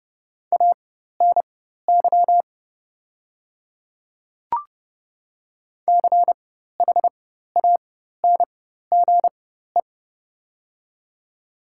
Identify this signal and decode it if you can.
Morse code, heard as a keyed tone, about 30 wpm, spaced as for 14 wpm, 700 Hz. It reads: ANY CHANGE
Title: 4x Character Spacing - Top 200 Words - 30wpm